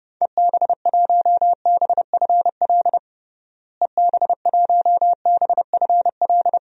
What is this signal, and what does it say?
Morse code, heard as a keyed tone, about 30 wpm, 700 Hz. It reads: EB1BFL EB1BFL
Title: DXCC-100 Call Signs - 6 Characters - Repeated 2x - 30wpm